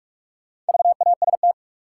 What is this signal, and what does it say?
Morse code, heard as a keyed tone, about 46 wpm, 700 Hz. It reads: VAST
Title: Sets of 4 Words - New Word Drill - 45wpm